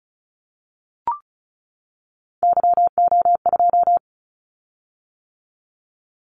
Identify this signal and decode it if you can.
Morse code, heard as a keyed tone, about 34 wpm, 700 Hz. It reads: YO2
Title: Callsign Prefixes for Top 25 DXCC - 35wpm